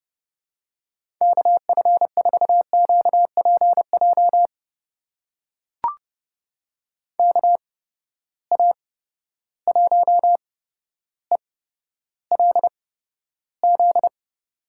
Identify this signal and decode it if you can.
Morse code, heard as a keyed tone, about 31 wpm, spaced as for 13 wpm, 700 Hz. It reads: KF4QPJ KA1ELZ
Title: USA Call Signs - 6 Characters - 8x Character Spacing - 30wpm